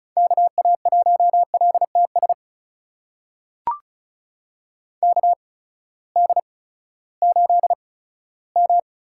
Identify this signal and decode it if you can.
Morse code, heard as a keyed tone, about 36 wpm, spaced as for 18 wpm, 700 Hz. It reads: KA1LTS KD8M
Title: USA Call Signs - 6 Characters - 8x Character Spacing - 35wpm